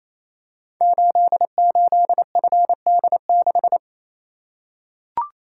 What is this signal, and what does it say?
Morse code, heard as a keyed tone, about 27 wpm, 700 Hz. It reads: 88FD6
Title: Code Groups - 5 Letter:Numbers - 28wpm